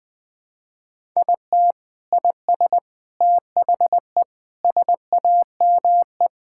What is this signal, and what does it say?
Morse code, heard as a keyed tone, about 20 wpm, 700 Hz. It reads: IT IS THE SAME
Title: Sentences with 6-Word Phrases - Spoken Beginning - 20wpm